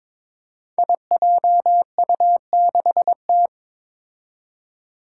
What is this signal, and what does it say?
Morse code, heard as a keyed tone, about 23 wpm, 700 Hz. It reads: IJU6T
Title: Five Challenging Letter-Number Combinations 22wpm